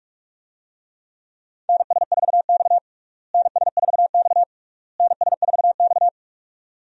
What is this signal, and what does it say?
Morse code, heard as a keyed tone, about 41 wpm, 700 Hz. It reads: NS4X NS4X NS4X
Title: USA Call Signs - 4 Characters - Repeated 3x - 45wpm